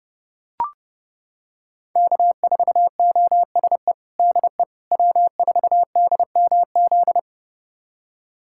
Morse code, heard as a keyed tone, about 30 wpm, 700 Hz. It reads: K4OSE DE W4DMZ
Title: USA Callsign DE Callsign - 5 Characters - 30wpm